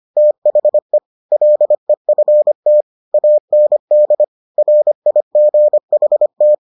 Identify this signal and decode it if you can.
Morse code, heard as a keyed tone, about 25 wpm, 600 Hz. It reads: THE LEFT AND RIGHT